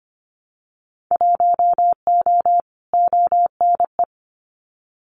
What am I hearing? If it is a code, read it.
Morse code, heard as a keyed tone, about 25 wpm, 700 Hz. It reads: 1O ONE
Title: Winter Field Day Exchanges 25wpm